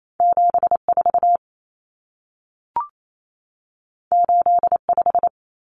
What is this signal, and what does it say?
Morse code, heard as a keyed tone, about 28 wpm, 700 Hz. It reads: 74 85
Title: Numbers - 2-Digit 28wpm